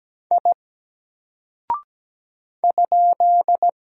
Morse code, heard as a keyed tone, about 17 wpm, 700 Hz. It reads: I ?
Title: Single Character (Letter:Number:Punctuation:Prosign) 17wpm